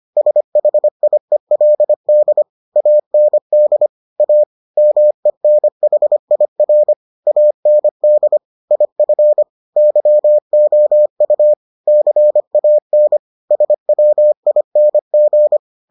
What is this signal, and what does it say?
Morse code, heard as a keyed tone, about 25 wpm, 600 Hz. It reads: SHIELD AND A MENHIR AND IF YOU CAN SWING